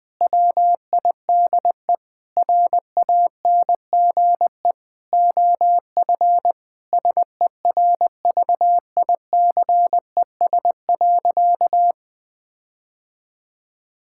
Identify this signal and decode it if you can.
Morse code, heard as a keyed tone, about 21 wpm, 700 Hz. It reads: WIDE RANGE OF SERVICES.
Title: Sentences with 4-Word Phrases - 20wpm